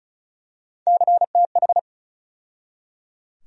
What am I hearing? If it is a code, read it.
Morse code, heard as a keyed tone, about 36 wpm, 700 Hz. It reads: CTH
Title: USA Call Sign Postfixes - 3 Letters 35wpm